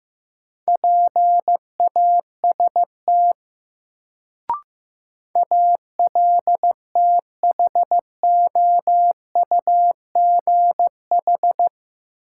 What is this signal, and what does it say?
Morse code, heard as a keyed tone, about 15 wpm, 700 Hz. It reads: PAST ALTHOUGH
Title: Sentences from Top 600 Words - New Word Drill - 15wpm